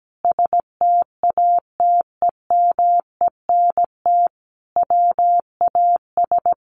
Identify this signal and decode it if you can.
Morse code, heard as a keyed tone, about 17 wpm, 700 Hz. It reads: STATEMENT WAS
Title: Sentences from Top 1200 Words 17wpm